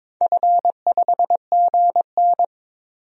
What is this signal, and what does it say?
Morse code, heard as a keyed tone, about 21 wpm, 700 Hz. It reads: F5GN
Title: DXCC-100 Call Signs - 4 Characters - Repeated 3x - 22wpm